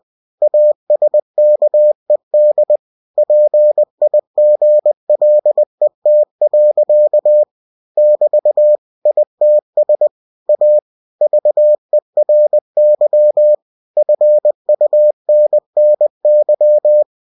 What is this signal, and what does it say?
Morse code, heard as a keyed tone, about 20 wpm, 600 Hz. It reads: ASKED PIGLET. <BT> ITS A VERY FUNNY
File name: cw-books / winnie-the-pooh / winnie-20wpm-page-0022